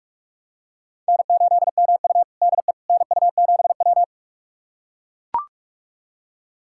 Morse code, heard as a keyed tone, about 44 wpm, 700 Hz. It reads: N8MU DE NU7W
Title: USA Callsign DE Callsign - 4 Characters - 45wpm